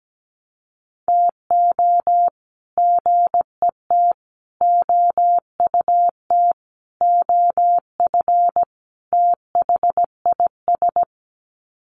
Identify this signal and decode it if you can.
Morse code, heard as a keyed tone, about 17 wpm, 700 Hz. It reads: TO GET OUT OF THIS
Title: Sets of 5 Words Encore2 17wpm